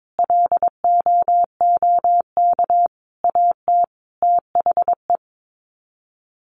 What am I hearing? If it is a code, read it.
Morse code, heard as a keyed tone, about 22 wpm, 700 Hz. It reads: LOOK AT THE
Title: Sets of 5 Words 22wpm